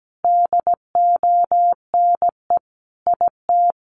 Morse code, heard as a keyed tone, about 17 wpm, 700 Hz. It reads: DONE IT
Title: Sentences from Top 1100 Words 17wpm